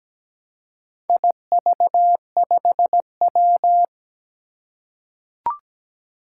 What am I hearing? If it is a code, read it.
Morse code, heard as a keyed tone, about 17 wpm, 700 Hz. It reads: IV5W
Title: Four Challenging Letter-Number Combinations 17wpm